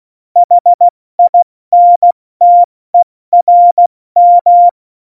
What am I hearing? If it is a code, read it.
Morse code, heard as a keyed tone, about 16 wpm, spaced as for 14 wpm, 700 Hz. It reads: HINTERM
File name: Morsekurs / Witze / Witz 16. April 2019, 70 F80